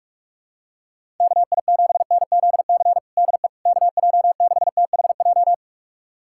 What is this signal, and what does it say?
Morse code, heard as a keyed tone, about 46 wpm, 700 Hz. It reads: KI7NZC DE KJ6THJ
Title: USA Callsign DE Callsign - 6 Characters - 45wpm